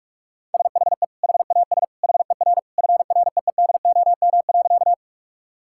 Morse code, heard as a keyed tone, about 41 wpm, 700 Hz. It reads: SHE HAS HER FREEDOM.
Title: Sentences from Top 1000 Words - Review Part 5 - 45wpm